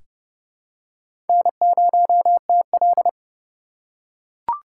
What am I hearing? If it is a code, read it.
Morse code, heard as a keyed tone, about 30 wpm, 700 Hz. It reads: N0TL
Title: USA Call Signs - 4 Characters - 2x Character Spacing - 30wpm